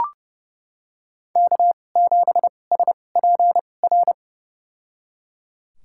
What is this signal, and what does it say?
Morse code, heard as a keyed tone, about 31 wpm, spaced as for 22 wpm, 700 Hz. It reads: K7SPR
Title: USA Call Signs - 5 Characters - 2x Character Spacing - 30wpm